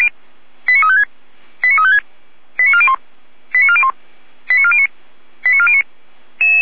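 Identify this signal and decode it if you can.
Five-tone selective call (ZVEI), heard as quick runs of five tones, tones of about 70 ms. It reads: 89356 89356 89401 89401 89409 89409
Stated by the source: ZVEI five-tone call, code 89104, received by SDR, from Feuerwehr Orlamünde.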